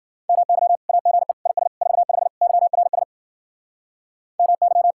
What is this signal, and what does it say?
Morse code, heard as a keyed tone, about 53 wpm, 700 Hz. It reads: MY AGE IS 45 YRS MY